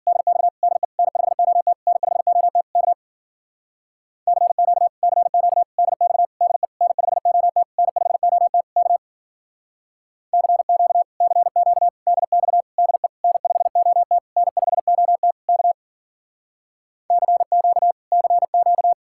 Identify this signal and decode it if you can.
Morse code, heard as a keyed tone, about 49 wpm, 700 Hz. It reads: DX DE N5OT N5OT K CQ CQ DX DE N5OT N5OT K CQ CQ DX DE N5OT N5OT K CQ CQ
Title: Speed-Racing Common QSO Phrases Encore - 40wpm